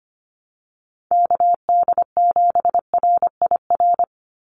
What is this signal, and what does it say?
Morse code, heard as a keyed tone, about 25 wpm, 700 Hz. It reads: KD7RIR
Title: USA Call Signs 25wpm